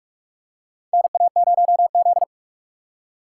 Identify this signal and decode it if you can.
Morse code, heard as a keyed tone, about 48 wpm, 700 Hz. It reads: NA0Z
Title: USA Call Signs - 4 Characters - 6x Character Spacing - 45wpm